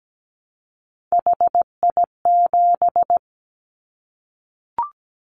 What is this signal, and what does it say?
Morse code, heard as a keyed tone, about 17 wpm, 700 Hz. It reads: HI7
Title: Callsign Prefixes for Top 100 DXCC - 17wpm